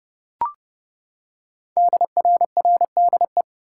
Morse code, heard as a keyed tone, about 28 wpm, 700 Hz. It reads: DRRDE
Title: Code Groups - 5 Letters - 30wpm